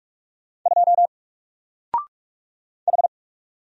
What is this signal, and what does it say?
Morse code, heard as a keyed tone, about 44 wpm, 700 Hz. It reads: 2 H
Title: Single Character (Letter:Number:Punctuation:Prosign) 45wpm